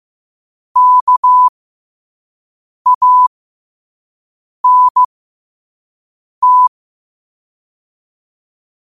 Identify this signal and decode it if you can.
Morse code, heard as a keyed tone, about 15 wpm, spaced as for 5 wpm, 1000 Hz. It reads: KANT
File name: qotdfiles / 5wpm